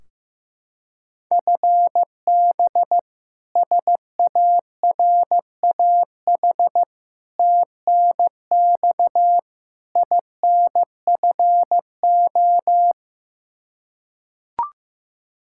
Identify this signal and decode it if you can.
Morse code, heard as a keyed tone, about 15 wpm, 700 Hz. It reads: FB SARAH TNX INFO
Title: K1USN SST QSO Practice 15wpm